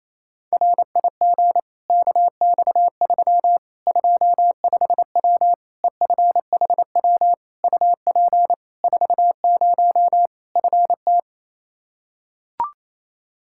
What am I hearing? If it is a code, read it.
Morse code, heard as a keyed tone, about 28 wpm, 700 Hz. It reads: RIG KX3 25W EFHW UP 40 FT
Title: Practice QSO - Exchange 2 - Info - 28wpm